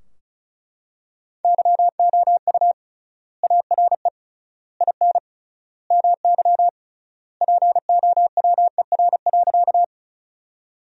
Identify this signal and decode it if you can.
Morse code, heard as a keyed tone, about 36 wpm, 700 Hz. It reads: YOU ARE IN MY POWER.
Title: 3x Word-Spacing - Sentences from Top 500 Words - 35wpm